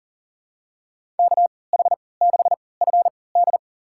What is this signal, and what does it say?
Morse code, heard as a keyed tone, about 40 wpm, spaced as for 23 wpm, 700 Hz. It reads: KH6FD
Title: USA Call Signs - 5 Characters - 3x Character Spacing - 40wpm